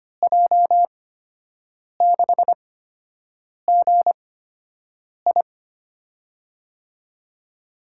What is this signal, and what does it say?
Morse code, heard as a keyed tone, about 25 wpm, spaced as for 7 wpm, 700 Hz. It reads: J6GI
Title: USA Call Signs - 5 Characters - 8x Character Spacing - 25wpm